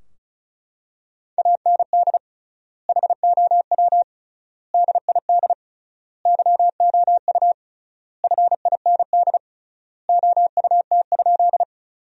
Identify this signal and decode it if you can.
Morse code, heard as a keyed tone, about 36 wpm, 700 Hz. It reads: AND HOW DID YOU FIND OUT?
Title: 3x Word-Spacing - Sentences from Top 200 Words - 35wpm